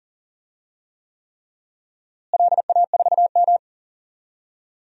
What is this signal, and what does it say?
Morse code, heard as a keyed tone, about 38 wpm, 700 Hz. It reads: LA4M
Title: DXCC-100 Call Signs - 4 Characters - Repeated 3x - 40wpm